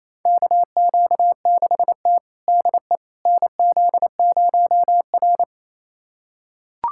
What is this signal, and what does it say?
Morse code, heard as a keyed tone, about 28 wpm, 700 Hz. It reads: KQ6T DE NZ0R
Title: USA Callsign DE Callsign - 4 Characters - 28wpm